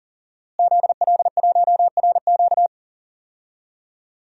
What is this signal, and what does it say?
Morse code, heard as a keyed tone, about 40 wpm, 700 Hz. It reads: ZL1RQ
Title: DXCC-100 Call Signs - 5 Characters - Repeated 2x - 40wpm